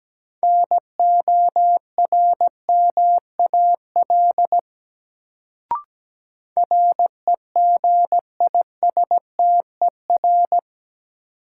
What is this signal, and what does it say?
Morse code, heard as a keyed tone, about 17 wpm, 700 Hz. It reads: NORMAL REGISTER
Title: Sentences from Top 1200 Words - New Word Drill Alt - 17wpm